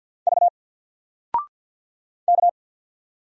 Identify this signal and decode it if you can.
Morse code, heard as a keyed tone, about 49 wpm, 700 Hz. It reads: V K
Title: Single Character (Letter:Number:Punctuation:Prosign) Alt 50wpm